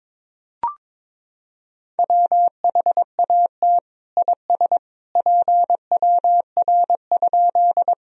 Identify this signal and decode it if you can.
Morse code, heard as a keyed tone, about 21 wpm, 700 Hz. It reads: WHAT IS PWR?
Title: Sentences from Top 500 Words and Abbreviations 22wpm